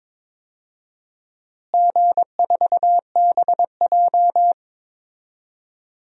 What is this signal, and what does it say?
Morse code, heard as a keyed tone, about 22 wpm, 700 Hz. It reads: G4BJ
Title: DXCC-100 Call Signs - 4 Characters - Repeated 2x - 22wpm